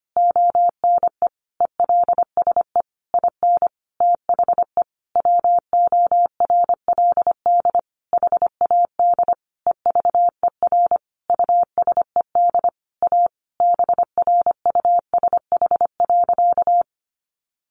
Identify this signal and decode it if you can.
Morse code, heard as a keyed tone, about 25 wpm, 700 Hz. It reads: ONE ELSE IN THE WORLD HAD EVER USED A BRUSH.
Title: Sentences from Top 1100 Words 25wpm